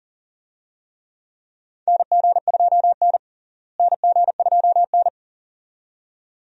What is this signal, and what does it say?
Morse code, heard as a keyed tone, about 37 wpm, 700 Hz. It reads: NG2N NG2N
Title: USA Call Signs - 4 Characters - Repeated 2x - 40wpm